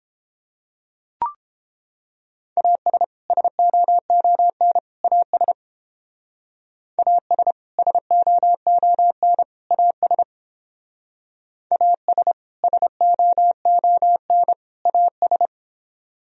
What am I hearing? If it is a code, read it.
Morse code, heard as a keyed tone, about 30 wpm, 700 Hz. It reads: AS SOON AS AS SOON AS AS SOON AS